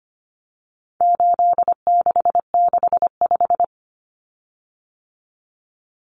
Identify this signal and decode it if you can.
Morse code, heard as a keyed tone, about 25 wpm, 700 Hz. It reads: 8665
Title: Common Cut Numbers - 4-Digit 25wpm